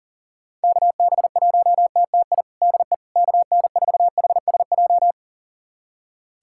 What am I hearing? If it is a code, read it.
Morse code, heard as a keyed tone, about 40 wpm, 700 Hz. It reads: KB1TTI DE KN4HSJ
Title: USA Callsign DE Callsign 40wpm